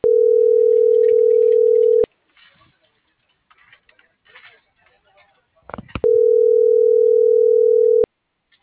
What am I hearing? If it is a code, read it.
Ringback tone (North America), 440 + 480 Hz.